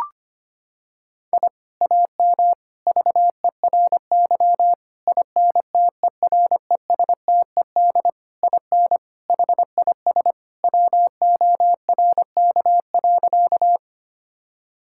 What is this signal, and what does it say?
Morse code, heard as a keyed tone, about 25 wpm, 700 Hz. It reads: I AM VERY INTERESTED IN HIS WORK.
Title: Sentences from Top 1000 Words - Review Part 6 - 25wpm